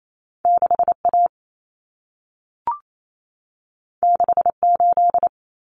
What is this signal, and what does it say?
Morse code, heard as a keyed tone, about 28 wpm, 700 Hz. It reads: 6A 68
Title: Common Cut Numbers - 2-Digit 28wpm